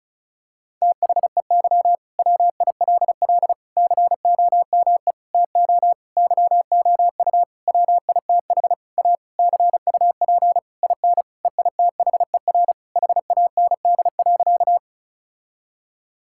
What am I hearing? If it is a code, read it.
Morse code, heard as a keyed tone, about 36 wpm, 700 Hz. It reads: THEY WILL COME TO YOU WITH A CUP IN EITHER HAND.
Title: Sentences from Top 1600 Words 35wpm